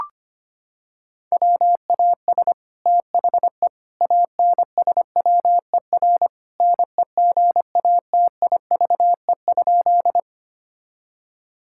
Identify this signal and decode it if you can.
Morse code, heard as a keyed tone, about 25 wpm, 700 Hz. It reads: WAS THE ANSWER NEGATIVE?
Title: Sentences from Top 1000 Words - Review Part 6 - 25wpm